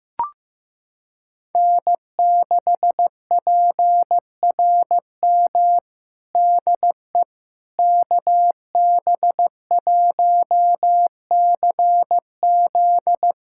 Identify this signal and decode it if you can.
Morse code, heard as a keyed tone, about 16 wpm, 700 Hz. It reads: N6PRM DE KB1CZ
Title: USA Callsign DE Callsign - 5 Characters - 15wpm